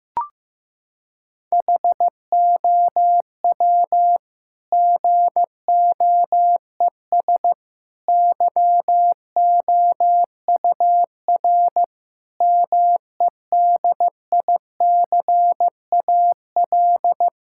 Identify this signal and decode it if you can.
Morse code, heard as a keyed tone, about 15 wpm, 700 Hz. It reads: HOW GOES YOUR MEDICAL
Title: Sentences from Top 1000 Words - Review Part 5 - 15wpm